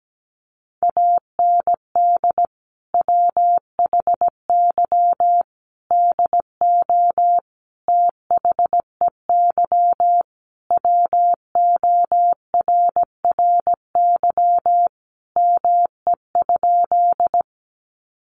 Morse code, heard as a keyed tone, about 17 wpm, 700 Hz. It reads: AND WHY DO THEY WORRY ME?
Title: Sentences from Top 1100 Words 17wpm